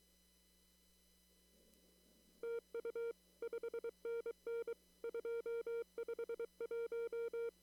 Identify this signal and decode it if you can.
Morse code, heard as a keyed tone, about 23 wpm, 460 Hz. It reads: TU 5NN 251
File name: CW / kt5j wpx cw-op-k5pi